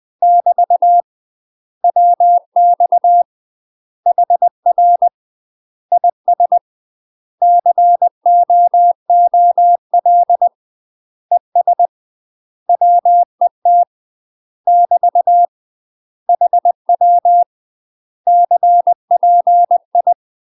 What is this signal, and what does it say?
Morse code, heard as a keyed tone, about 20 wpm, 700 Hz. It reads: <BT> WX HR IS COOL ES WET <BT> HW CPI
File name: UK MASTER QSOs TRAINING / 20WPM / UK MASTER QSOs 0053